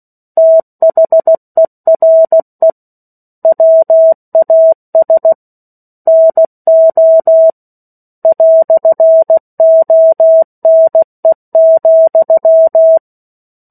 Morse code, heard as a keyed tone, about 16 wpm, 650 Hz. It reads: THERE WAS NO "ONE,
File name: 03-Chapter-03 16 WPM